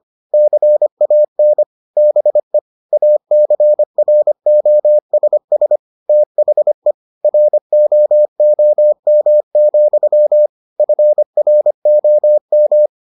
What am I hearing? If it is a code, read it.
Morse code, heard as a keyed tone, about 25 wpm, 600 Hz. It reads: CAN BE ACROSS THE ROOM, FROM